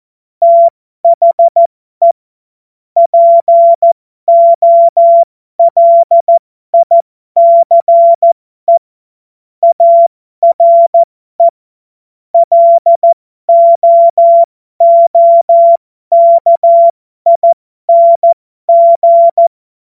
Morse code, heard as a keyed tone, about 14 wpm, spaced as for 12 wpm, 680 Hz. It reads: THE POLICE ARE LOOKING
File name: Morsekurs / Witze